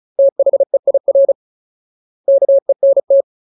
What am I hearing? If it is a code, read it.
Morse code, heard as a keyed tone, about 34 wpm, 550 Hz. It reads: THEIR KENT
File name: Ultimate Word List / 35WPM Ultimate